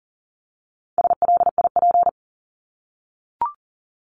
Morse code, heard as a keyed tone, about 40 wpm, 700 Hz. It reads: SLIP
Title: Top 1751-2000 Words 40wpm